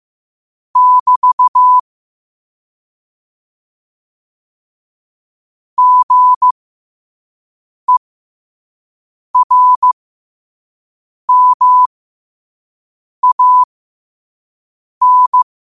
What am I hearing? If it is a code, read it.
Morse code, heard as a keyed tone, about 15 wpm, spaced as for 5 wpm, 1000 Hz. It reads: <BT> GERMAN
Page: New quotes every day in morse code at 5 Words per minute.